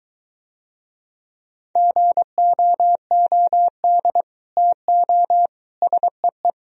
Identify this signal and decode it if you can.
Morse code, heard as a keyed tone, about 23 wpm, 700 Hz. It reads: GOOD TO SEE